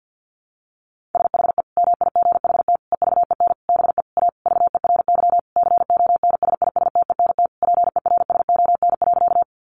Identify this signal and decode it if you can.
Morse code, heard as a keyed tone, about 50 wpm, 700 Hz. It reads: SHE MIGHT EVEN BE A VERY CONSISTENT PERSON.
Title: Sentences from Top 1500 Words 50wpm